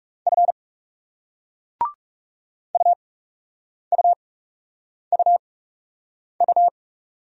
Morse code, heard as a keyed tone, about 41 wpm, spaced as for 8 wpm, 700 Hz. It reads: F UUUU